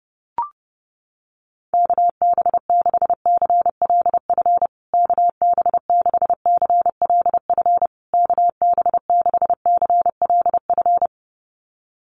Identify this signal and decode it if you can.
Morse code, heard as a keyed tone, about 30 wpm, 700 Hz. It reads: KB6CLF KB6CLF KB6CLF
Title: USA Call Signs Repeated 3x 30wpm